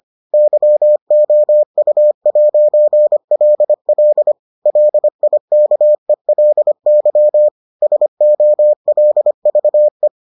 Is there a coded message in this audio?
Yes: Morse code from 0.3 to 10.1 s, heard as a keyed tone, about 25 wpm, 600 Hz. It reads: YOU'LL LIKELY SOLVE